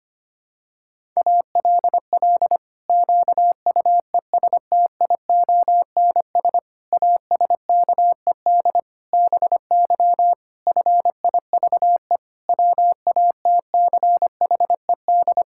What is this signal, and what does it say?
Morse code, heard as a keyed tone, about 25 wpm, 700 Hz. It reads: ALL QUESTIONS ASKED BY FIVE WATCHED
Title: Pangram Challenge v2 - 25wpm